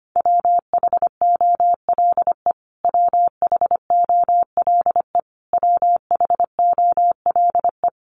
Morse code, heard as a keyed tone, about 25 wpm, 700 Hz. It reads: WHOLE WHOLE WHOLE
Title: Top 500 Words Repeated 25wpm